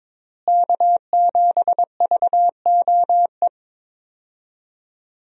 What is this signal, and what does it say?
Morse code, heard as a keyed tone, about 22 wpm, 700 Hz. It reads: K7VOE
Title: USA Call Signs - 5 Characters - Repeated 2x - 22wpm